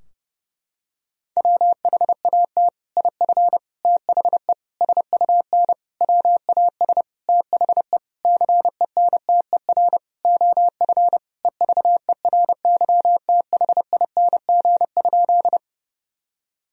Morse code, heard as a keyed tone, about 30 wpm, 700 Hz. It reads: WHAT IF THE SUN WAS THE CENTER OF EVERYTHING?
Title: Sentences from Top 1000 Words - Review Part 2 - 30wpm